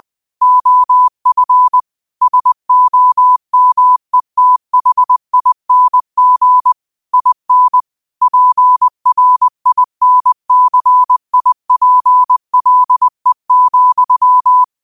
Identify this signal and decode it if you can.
Morse code, heard as a keyed tone, about 20 wpm, 1000 Hz. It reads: OF SOMETHING IN PRINCIPLE,